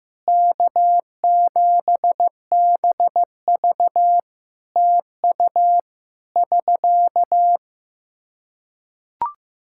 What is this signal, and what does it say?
Morse code, heard as a keyed tone, about 15 wpm, 700 Hz. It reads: K7BV TU <SK>
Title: Practice QSO - Exchange 4 - Ending - 15wpm